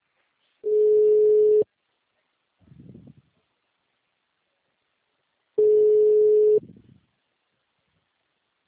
Ringback tone (Europe, 425 Hz).